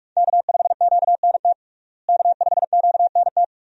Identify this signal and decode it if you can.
Morse code, heard as a keyed tone, about 42 wpm, 700 Hz. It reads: K5QNT K5QNT
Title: USA Call Signs - 5 Characters - Repeated 2x - 45wpm